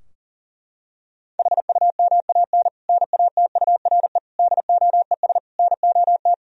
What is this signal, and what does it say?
Morse code, heard as a keyed tone, about 40 wpm, 700 Hz. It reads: HUMAN NATURE DOES NOT
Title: Sentences from Top 1000 Words - Review Part 3 - 40wpm